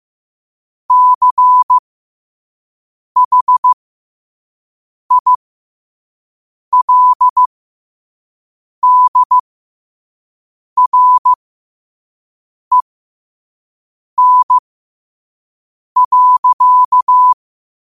Morse code, heard as a keyed tone, about 15 wpm, spaced as for 5 wpm, 1000 Hz. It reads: CHILDREN.
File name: qotdfiles / 5wpm